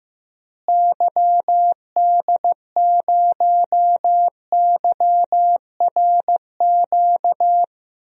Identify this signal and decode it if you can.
Morse code, heard as a keyed tone, about 15 wpm, 700 Hz. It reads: YD0YRQ
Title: DXCC-100 Call Signs - 6 Characters - Repeated 3x - 15wpm